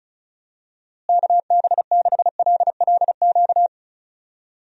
Morse code, heard as a keyed tone, about 34 wpm, 700 Hz. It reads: KB6LLQ
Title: USA Call Signs - 6 Characters - Repeated 3x - 35wpm